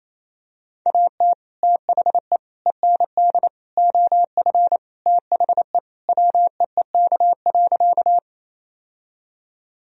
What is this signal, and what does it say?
Morse code, heard as a keyed tone, about 27 wpm, 700 Hz. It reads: AT THE END OF THE WEEK.
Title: Sentences from Top 1000 Words - Review Part 2 - 28wpm